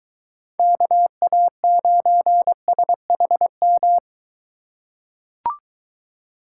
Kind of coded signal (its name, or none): Morse code